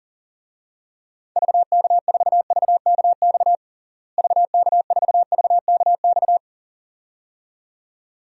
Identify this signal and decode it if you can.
Morse code, heard as a keyed tone, about 40 wpm, 700 Hz. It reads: VK4VKX VK4VKX
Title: DXCC-100 Call Signs - 6 Characters - Repeated 2x - 40wpm